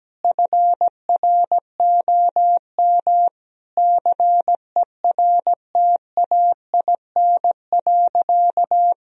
Morse code, heard as a keyed tone, about 17 wpm, 700 Hz. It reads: FROM CERTAIN.
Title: Sentences from Top 1000 Words - Review Part 2 - 17wpm